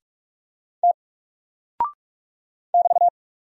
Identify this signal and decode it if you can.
Morse code, heard as a keyed tone, about 44 wpm, 700 Hz. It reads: T <BT>
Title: Single Character (Letter:Number:Punctuation:Prosign) 45wpm